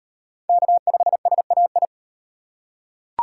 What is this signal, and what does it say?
Morse code, heard as a keyed tone, about 33 wpm, 700 Hz. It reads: K5SAI